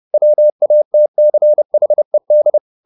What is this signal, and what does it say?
Morse code, heard as a keyed tone, about 30 wpm, 600 Hz. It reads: WATCHED
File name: the old man and the sea / the old man and the sea 1 30wpm